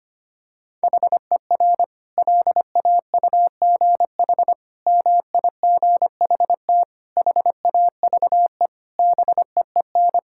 Morse code, heard as a keyed tone, about 25 wpm, 700 Hz. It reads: HER LAUGH MIGHT HAVE BEEN
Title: Sentences from Top 1000 Words - Review Part 5 - 25wpm